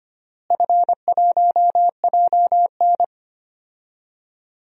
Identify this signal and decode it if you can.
Morse code, heard as a keyed tone, about 25 wpm, 700 Hz. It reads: F1JN
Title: DXCC-100 Call Signs - 4 Characters - Repeated 3x - 25wpm